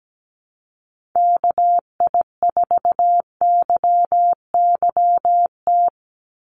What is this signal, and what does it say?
Morse code, heard as a keyed tone, about 17 wpm, 700 Hz. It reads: KI4YYT
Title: USA Call Signs 17wpm